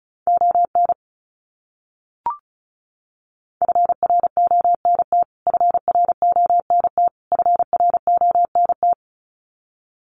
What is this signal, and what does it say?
Morse code, heard as a keyed tone, about 34 wpm, 700 Hz. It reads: ON FRONT FRONT FRONT
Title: Top 500 Words Repeated 35wpm